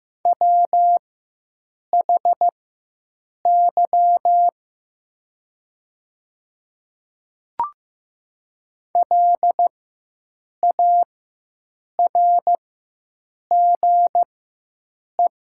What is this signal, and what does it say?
Morse code, heard as a keyed tone, about 15 wpm, spaced as for 7 wpm, 700 Hz. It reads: WHY LARGE
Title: 4x Character Spacing - Top 200 Words - 15wpm